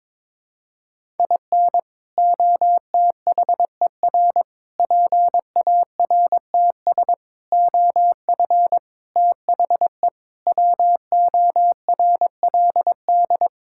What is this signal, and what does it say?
Morse code, heard as a keyed tone, about 22 wpm, 700 Hz. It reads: IN OTHER PARTS OF THE WORLD
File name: practice / 22wpm / Sets of 6 Words - 22wpm